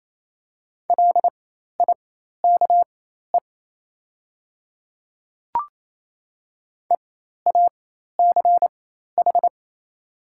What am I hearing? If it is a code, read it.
Morse code, heard as a keyed tone, about 28 wpm, spaced as for 13 wpm, 700 Hz. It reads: LIKE EACH
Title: 4x Character Spacing - Top 100 Words - 28wpm